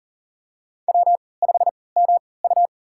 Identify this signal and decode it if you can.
Morse code, heard as a keyed tone, about 40 wpm, spaced as for 23 wpm, 700 Hz. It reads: W5MU
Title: USA Call Signs - 4 Characters - 3x Character Spacing - 40wpm